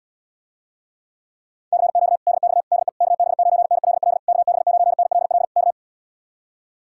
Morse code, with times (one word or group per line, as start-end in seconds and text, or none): 1.7-2.2 CQ
2.3-2.6 DX
2.7-2.9 DE
3.0-4.2 KB9NLL
4.3-5.4 KB9NLL
5.6-5.7 K